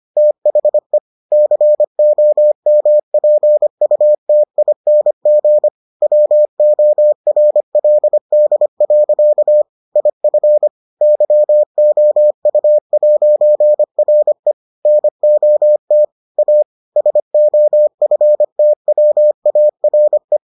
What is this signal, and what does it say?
Morse code, heard as a keyed tone, about 25 wpm, 600 Hz. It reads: THE COMPUTING WORLD. IF YOU'RE NOT A SOFTWARE